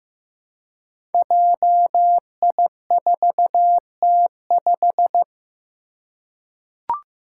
Morse code, heard as a keyed tone, about 15 wpm, 700 Hz. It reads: JI4T5
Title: Five Challenging Letter-Number Combinations 15wpm